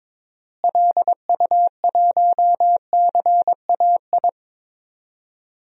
Morse code, heard as a keyed tone, about 22 wpm, 700 Hz. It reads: LU1CAI
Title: DXCC-100 Call Signs - 6 Characters - Repeated 3x - 22wpm